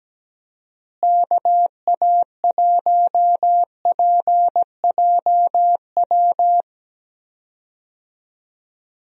Morse code, heard as a keyed tone, about 17 wpm, 700 Hz. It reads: KA1PJW
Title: USA Call Signs - 6 Characters - Repeated 3x - 17wpm